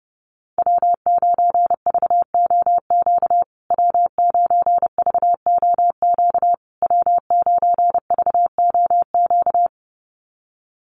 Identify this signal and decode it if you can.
Morse code, heard as a keyed tone, about 30 wpm, 700 Hz. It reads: W9VOQ W9VOQ W9VOQ
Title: USA Call Signs Repeated 3x 30wpm